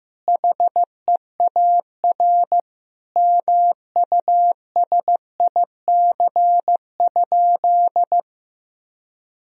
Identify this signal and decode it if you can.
Morse code, heard as a keyed tone, about 15 wpm, 700 Hz. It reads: HEAR MUSIC?
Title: Sentences from Top 1000 Words - Review Part 1 - 15wpm